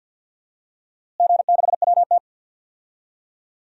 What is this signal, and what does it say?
Morse code, heard as a keyed tone, about 52 wpm, 700 Hz. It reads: G6WT
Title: DXCC-100 Call Signs - 4 Characters - Repeated 2x - 50wpm